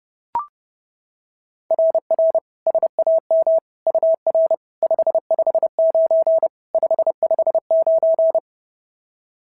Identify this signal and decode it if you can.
Morse code, heard as a keyed tone, about 30 wpm, 690 Hz. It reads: RR SAM UR 559 559
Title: SOTA QSO Practice 30wpm